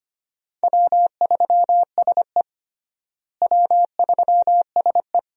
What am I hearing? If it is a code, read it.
Morse code, heard as a keyed tone, about 25 wpm, 700 Hz. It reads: W3SE W3SE
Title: USA Call Signs - 4 Characters - Repeated 2x - 25wpm